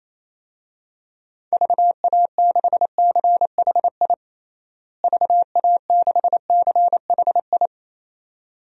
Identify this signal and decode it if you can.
Morse code, heard as a keyed tone, about 28 wpm, 700 Hz. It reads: VA6CHI VA6CHI
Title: DXCC-100 Call Signs - 6 Characters - Repeated 2x - 28wpm